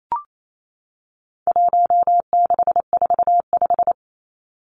Morse code, heard as a keyed tone, about 28 wpm, 700 Hz. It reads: 1645
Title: Numbers - 4-Digit 28wpm